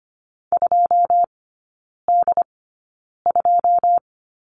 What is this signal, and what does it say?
Morse code, heard as a keyed tone, about 25 wpm, spaced as for 9 wpm, 700 Hz. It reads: 2D2
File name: Lesson 14e - Number 2 - Three Characters 25wpm